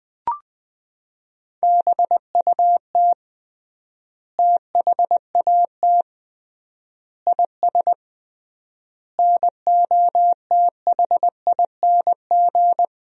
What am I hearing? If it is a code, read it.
Morse code, heard as a keyed tone, about 20 wpm, 700 Hz. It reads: BUT THAT IS NOTHING
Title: 3x Word-Spacing - Sentences from Top 500 Words - 20wpm